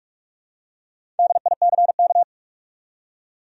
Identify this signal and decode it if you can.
Morse code, heard as a keyed tone, about 46 wpm, 700 Hz. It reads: DICK
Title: USA Common Names - Max 4 Letters - 45wpm